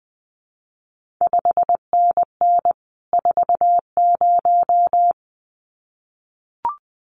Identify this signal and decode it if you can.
Morse code, heard as a keyed tone, about 20 wpm, 700 Hz. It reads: 5NN 40
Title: CQ WW DX Contest Exchange Encore - 20wpm